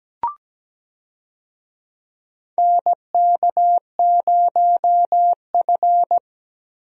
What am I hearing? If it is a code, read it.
Morse code, heard as a keyed tone, about 17 wpm, 700 Hz. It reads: NK0F
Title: USA Call Signs - 4 Characters - Repeated 3x - 17wpm